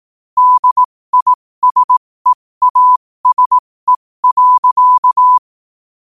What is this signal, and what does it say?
Morse code, heard as a keyed tone, about 18 wpm, spaced as for 15 wpm, 1000 Hz. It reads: DISEASE.
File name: qotdfiles / 15wpm